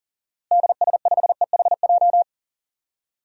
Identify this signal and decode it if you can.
Morse code, heard as a keyed tone, about 37 wpm, 700 Hz. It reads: DS5EHJ
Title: DXCC-100 Call Signs - 6 Characters - Repeated 2x - 40wpm